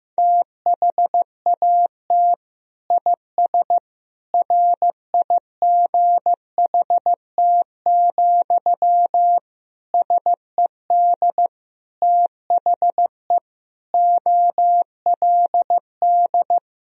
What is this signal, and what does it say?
Morse code, heard as a keyed tone, about 15 wpm, 700 Hz. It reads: THAT IS RIGHT, SED THE OLD
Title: Sentences from Top 200 Words and Abbreviations 15wpm